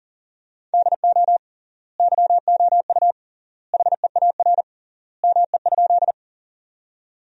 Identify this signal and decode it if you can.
Morse code, heard as a keyed tone, about 40 wpm, 700 Hz. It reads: DO YOU HEAR ME?
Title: 3x Word-Spacing - Sentences from Top 300 Words - 40wpm